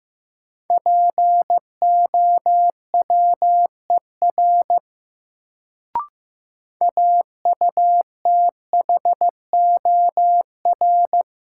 Morse code, heard as a keyed tone, about 15 wpm, 700 Hz. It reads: POWER AUTHOR
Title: Sets of 4 Words - New Word Drill - 15wpm